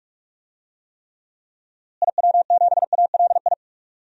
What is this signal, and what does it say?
Morse code, heard as a keyed tone, about 43 wpm, 700 Hz. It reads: IW7ALI
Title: DXCC-100 Call Signs - 6 Characters - Repeated 3x - 45wpm